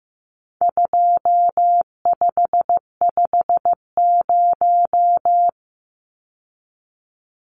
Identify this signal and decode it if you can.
Morse code, heard as a keyed tone, about 15 wpm, 700 Hz. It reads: 2550
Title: Numbers - 4-Digit 15wpm